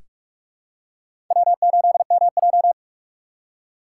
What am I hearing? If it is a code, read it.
Morse code, heard as a keyed tone, about 46 wpm, 700 Hz. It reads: W8MJ
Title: USA Call Signs - 4 Characters - 3x Character Spacing - 45wpm